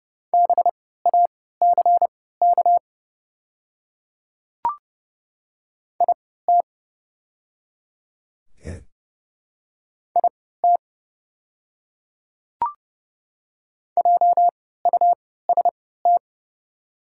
Morse code, heard as a keyed tone, about 30 wpm, spaced as for 17 wpm, 690 Hz. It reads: BACK IT IT JUST
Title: 3x Character Spacing - Top 500 Words - 30wpm